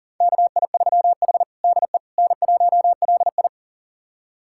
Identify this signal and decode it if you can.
Morse code, heard as a keyed tone, about 40 wpm, 700 Hz. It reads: KI3H DE N1LI
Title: USA Callsign DE Callsign - 4 Characters - 40wpm